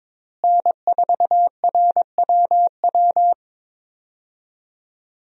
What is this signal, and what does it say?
Morse code, heard as a keyed tone, about 22 wpm, 700 Hz. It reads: N4RWW
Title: USA Call Signs - 5 Characters - Repeated 2x - 22wpm